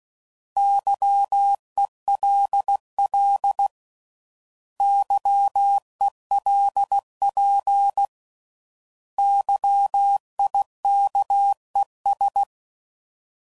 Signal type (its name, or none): Morse code